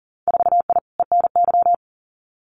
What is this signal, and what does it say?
Morse code, heard as a keyed tone, about 41 wpm, 700 Hz. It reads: 4I ENY
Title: Winter Field Day Exchanges Encore 40wpm